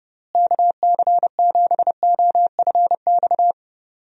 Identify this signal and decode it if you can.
Morse code, heard as a keyed tone, about 30 wpm, 700 Hz. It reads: KC7OFX
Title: USA Call Signs - 6 Characters - Repeated 3x - 30wpm